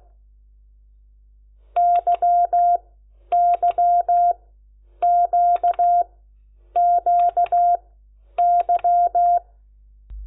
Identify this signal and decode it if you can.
Morse code, heard as a keyed tone, about 16 wpm, spaced as for 9 wpm, 700 Hz. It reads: YYQQY
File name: Lecciones / CLASE Nº10